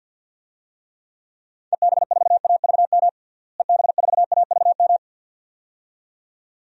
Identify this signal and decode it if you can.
Morse code, heard as a keyed tone, about 50 wpm, 700 Hz. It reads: EB4AVM EB4AVM
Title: DXCC-100 Call Signs - 6 Characters - Repeated 2x - 50wpm